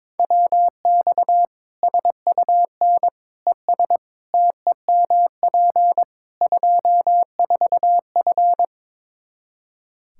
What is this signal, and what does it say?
Morse code, heard as a keyed tone, about 22 wpm, 700 Hz. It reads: WX SUN ES TEMP 24F
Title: Practice QSO - Exchange 2 - Info - 22wpm